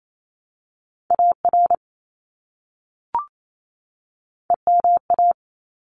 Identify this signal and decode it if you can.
Morse code, heard as a keyed tone, about 28 wpm, 700 Hz. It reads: AR EMA
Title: ARRL Sections 28wpm